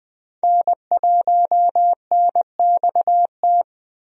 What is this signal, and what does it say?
Morse code, heard as a keyed tone, about 20 wpm, 700 Hz. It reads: N1NXT
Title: USA Call Signs - 5 Characters - Repeated 2x - 20wpm